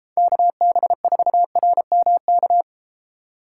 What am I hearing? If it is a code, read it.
Morse code, heard as a keyed tone, about 32 wpm, 700 Hz. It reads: KB4RMK